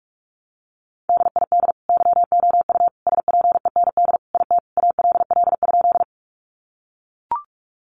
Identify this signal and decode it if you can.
Morse code, heard as a keyed tone, about 44 wpm, 700 Hz. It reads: DID YOU SPEND IT ALL?
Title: Sentences from Top 800 Words 45wpm